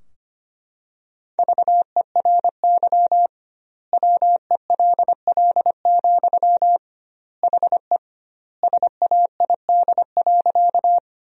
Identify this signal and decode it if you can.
Morse code, heard as a keyed tone, about 25 wpm, 700 Hz. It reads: VERY WELL, HE SAID.
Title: 2x Word-Spacing - Sentences from Top 200 Words - 25wpm